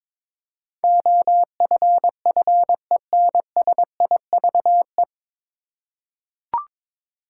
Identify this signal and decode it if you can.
Morse code, heard as a keyed tone, about 22 wpm, 690 Hz. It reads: OFFENSIVE
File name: 9 Letter Words - First 3 Letters Given - 22wpm